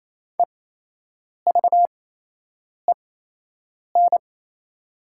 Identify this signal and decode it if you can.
Morse code, heard as a keyed tone, about 28 wpm, spaced as for 8 wpm, 690 Hz. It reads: EVEN
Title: 8x Character Spacing - Top 200 Words - 28wpm